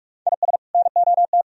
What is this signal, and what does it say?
Morse code, heard as a keyed tone, about 45 wpm, 690 Hz. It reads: IS NOT